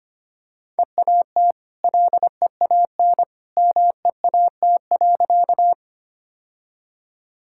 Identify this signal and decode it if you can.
Morse code, heard as a keyed tone, about 25 wpm, 700 Hz. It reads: EAT LEAN MEAT.
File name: Easy Sentences with Hints - 25wpm